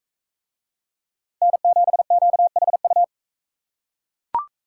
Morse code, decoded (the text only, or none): N7QHU